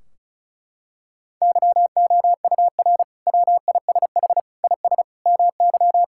Morse code, heard as a keyed tone, about 36 wpm, 700 Hz. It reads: YOUR WISH IS MY
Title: Sentences from Top 1600 Words 35wpm